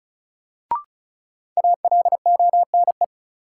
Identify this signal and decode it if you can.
Morse code, heard as a keyed tone, about 36 wpm, 700 Hz. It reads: ALONE
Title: Sentences from Top 1000 Words - New Word Drill - 35wpm